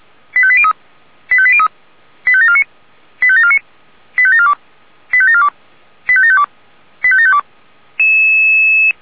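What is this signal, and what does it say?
Five-tone selective call (ZVEI), heard as quick runs of five tones, tones of about 70 ms. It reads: 85803 85803 85749 85749 85742 85742 85752 85752
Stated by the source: ZVEI five-tone call 85725 received on SDR.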